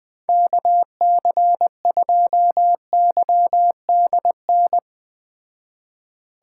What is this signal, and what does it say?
Morse code, heard as a keyed tone, about 19 wpm, 700 Hz. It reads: KC2YDN
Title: Speed Racing - USA 2x3 Call Signs 17wpm